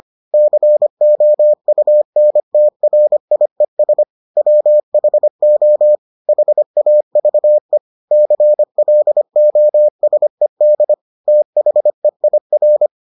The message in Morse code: COUNTRIES WHO HAVE CLOSED THEIR

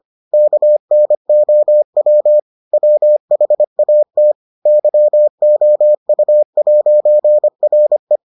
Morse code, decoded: KNOW WHAT YOU'RE

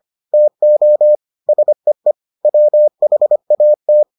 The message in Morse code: TO SEE WHAT

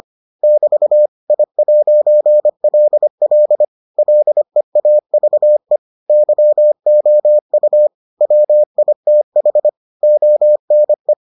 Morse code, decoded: <BT> I'LL LEAVE YOU WITH ONE

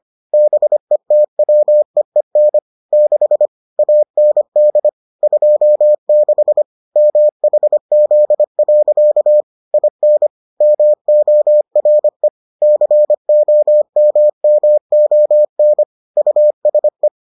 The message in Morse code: BETWEEN 6 AND 26 MHZ. IN MORE COMMON USE

